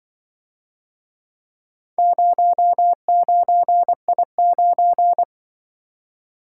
Morse code, heard as a keyed tone, about 23 wpm, 700 Hz. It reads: 09I9
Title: Speed-Racing - Four Letter-Number Combinations - 20wpm